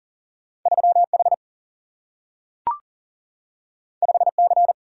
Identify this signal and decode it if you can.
Morse code, heard as a keyed tone, about 39 wpm, 700 Hz. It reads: 3 H 5 C
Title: Single-Letter Single-Number 40wpm